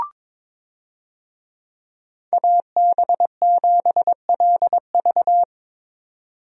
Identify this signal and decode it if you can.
Morse code, heard as a keyed tone, about 22 wpm, 700 Hz. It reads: AB7LV
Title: USA Call Signs - 5 Characters - 22wpm